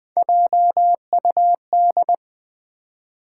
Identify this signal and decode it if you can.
Morse code, heard as a keyed tone, about 20 wpm, 700 Hz. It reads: JUD
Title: Build A Word - 1 Letter At A Time - 5-Letter Words - 20wpm